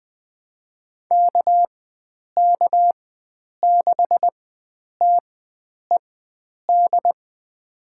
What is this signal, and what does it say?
Morse code, heard as a keyed tone, about 21 wpm, spaced as for 9 wpm, 700 Hz. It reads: KK6TED
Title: USA Call Signs - 6 Characters - 4x Character Spacing - 20wpm